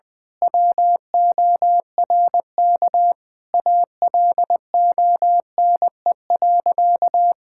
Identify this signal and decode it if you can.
Morse code, heard as a keyed tone, about 20 wpm, 700 Hz. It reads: WORK ALONE.
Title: Sentences from Top 1600 Words 20wpm